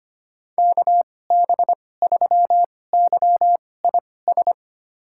Morse code, heard as a keyed tone, about 25 wpm, spaced as for 18 wpm, 700 Hz. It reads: KB3YIS
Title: USA Call Signs - 6 Characters - 2x Character Spacing - 25wpm